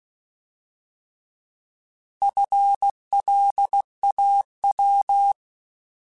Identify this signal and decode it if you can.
Morse code, heard as a keyed tone, about 16 wpm, 800 Hz. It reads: FLAW
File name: Alphabetical / Alphabetical mp3s / Fs @15